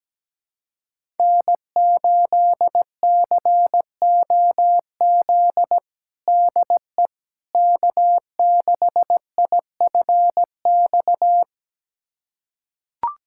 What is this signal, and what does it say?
Morse code, heard as a keyed tone, about 17 wpm, 700 Hz. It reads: N8COZ DE K6IFX
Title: USA Callsign DE Callsign - 5 Characters - 17wpm